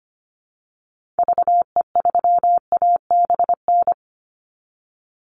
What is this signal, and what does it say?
Morse code, heard as a keyed tone, about 25 wpm, 700 Hz. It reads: VE3ABN
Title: Call Signs Repeated 2x - 25wpm